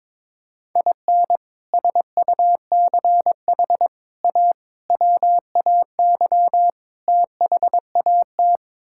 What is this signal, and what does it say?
Morse code, heard as a keyed tone, about 22 wpm, 700 Hz. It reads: IN SUCH A WAY THAT
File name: practice / 22wpm / Build A Phrase - 5-Word Phrases - 22wpm